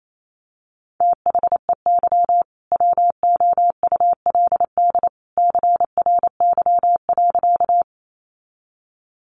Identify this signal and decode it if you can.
Morse code, heard as a keyed tone, about 28 wpm, 700 Hz. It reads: THEY WOULD CRY.
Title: Sentences from Top 500 Words 28wpm